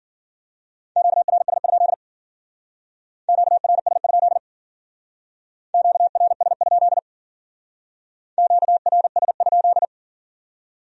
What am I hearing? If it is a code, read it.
Morse code, heard as a keyed tone, about 49 wpm, 700 Hz. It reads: QRS? QRS? QRS? QRS?
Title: Speed-Racing Q-Codes - 40wpm